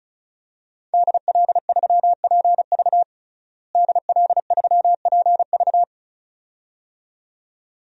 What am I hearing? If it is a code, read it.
Morse code, heard as a keyed tone, about 34 wpm, 700 Hz. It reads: DL3PV DL3PV
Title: DXCC-100 Call Signs - 5 Characters - Repeated 2x - 35wpm